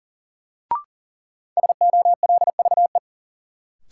Morse code, heard as a keyed tone, about 40 wpm, 700 Hz. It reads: SOLVE
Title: Sentences from Top 1000 Words - New Word Drill - 40wpm